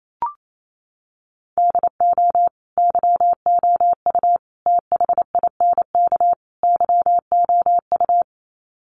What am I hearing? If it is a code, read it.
Morse code, heard as a keyed tone, about 28 wpm, 700 Hz. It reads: DO YOU THINK YOU
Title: Sets of 4 Words 28wpm